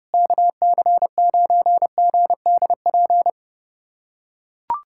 Morse code, heard as a keyed tone, about 30 wpm, 700 Hz. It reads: KC9GDP
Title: USA Callsign DE Callsign 30wpm